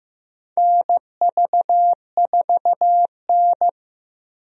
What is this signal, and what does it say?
Morse code, heard as a keyed tone, about 15 wpm, 700 Hz. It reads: NV4N
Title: USA Call Signs - 4 Characters - Repeated 2x - 15wpm